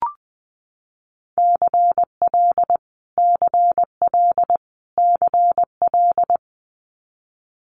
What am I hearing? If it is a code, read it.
Morse code, heard as a keyed tone, about 20 wpm, 700 Hz. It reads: CL CL CL
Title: Multi-letter Phonemes Repeated 3x 20wpm